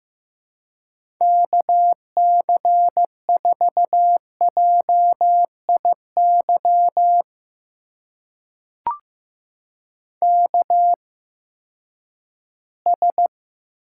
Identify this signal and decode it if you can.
Morse code, heard as a keyed tone, about 15 wpm, spaced as for 10 wpm, 700 Hz. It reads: KC4JIY KS
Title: USA Call Signs - 8x Character Spacing - 15wpm